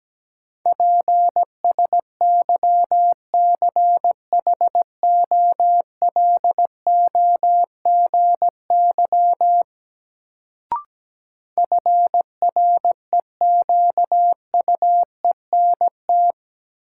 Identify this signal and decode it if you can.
Morse code, heard as a keyed tone, about 17 wpm, 700 Hz. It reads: PSYCHOLOGY FREQUENT
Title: Sentences from Top 1400 Words - New Word Drill Alt - 17wpm